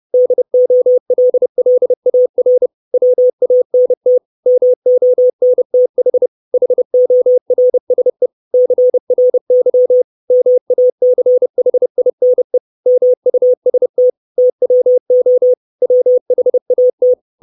Morse code, heard as a keyed tone, about 30 wpm, 500 Hz. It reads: DOLLAR WANT MONTH HORSE CRY MACHINE MUST TWO WHAT
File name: 632 Words 30WPM